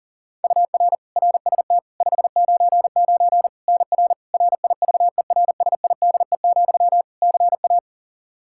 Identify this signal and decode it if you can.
Morse code, heard as a keyed tone, about 40 wpm, 700 Hz. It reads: UR RST 599 NR RIVERSIDE, CA
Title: Practice QSO - Exchange 1 - Beginning - 40wpm